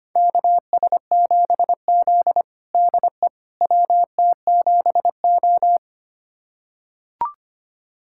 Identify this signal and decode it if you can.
Morse code, heard as a keyed tone, about 25 wpm, 700 Hz. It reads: KS7Z DE WT7O
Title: USA Callsign DE Callsign - 4 Characters - 25wpm